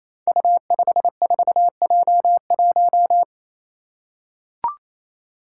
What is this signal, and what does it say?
Morse code, heard as a keyed tone, about 29 wpm, 700 Hz. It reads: U54J1
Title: Five Challenging Letter-Number Combinations 28wpm